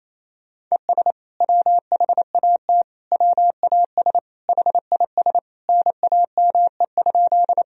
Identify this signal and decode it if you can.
Morse code, heard as a keyed tone, about 28 wpm, 700 Hz. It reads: ES WHAT WAS HIS NAME?
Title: Sentences from Top 200 Words and Abbreviations 28wpm